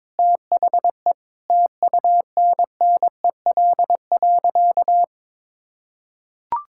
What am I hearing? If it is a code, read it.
Morse code, heard as a keyed tone, about 22 wpm, 700 Hz. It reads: THE TUNNEL.
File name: practice / 22wpm / Sentences with 8-Word Phrases - 22wpm